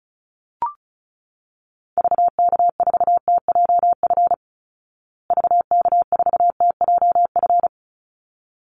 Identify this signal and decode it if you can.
Morse code, heard as a keyed tone, about 34 wpm, 700 Hz. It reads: VK4TJF VK4TJF
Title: Call Signs Repeated 3x - 35wpm